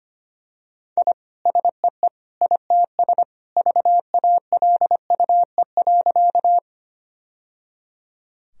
Morse code, heard as a keyed tone, about 25 wpm, 700 Hz. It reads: I SEE ITS VALUE.
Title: Sentences from Top 1000 Words - Review Part 3 - 25wpm